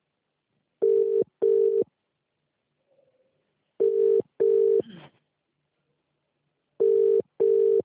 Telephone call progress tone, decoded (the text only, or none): ringback tone